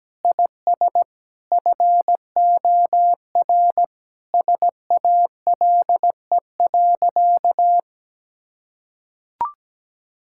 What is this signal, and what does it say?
Morse code, heard as a keyed tone, about 17 wpm, 700 Hz. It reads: IS FOR SALE.
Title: Sentences from Top 1600 Words 17wpm